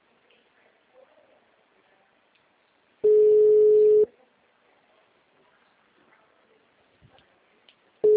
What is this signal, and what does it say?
Ringback tone (Europe, 425 Hz).